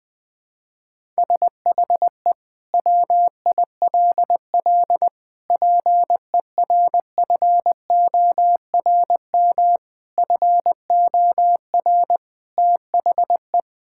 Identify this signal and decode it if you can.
Morse code, heard as a keyed tone, about 20 wpm, 700 Hz. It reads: SHE WILL PERFORM FOR THE
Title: Sentences from Top 1000 Words - Review Part 5 - 20wpm